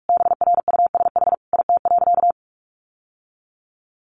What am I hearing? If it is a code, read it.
Morse code, heard as a keyed tone, about 44 wpm, 700 Hz. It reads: BRUSH IT.
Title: Sentences from Top 1100 Words 45wpm